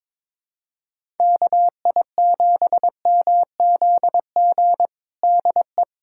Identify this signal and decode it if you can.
Morse code, heard as a keyed tone, about 21 wpm, 700 Hz. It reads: KI7MZG DE
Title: USA Callsign DE Callsign 22wpm